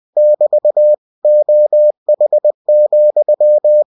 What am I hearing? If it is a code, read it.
Morse code, heard as a keyed tone, about 20 wpm, 600 Hz. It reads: <BT> OH,
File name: cw-books / winnie-the-pooh / winnie-20wpm-page-0053